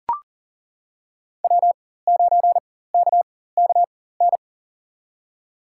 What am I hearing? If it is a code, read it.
Morse code, heard as a keyed tone, about 40 wpm, spaced as for 19 wpm, 700 Hz. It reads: W9KKN
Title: USA Call Signs - 4x Character Spacing - 40wpm